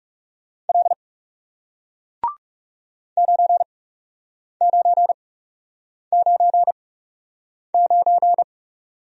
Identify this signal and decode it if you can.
Morse code, heard as a keyed tone, about 40 wpm, spaced as for 8 wpm, 700 Hz. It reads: L 9999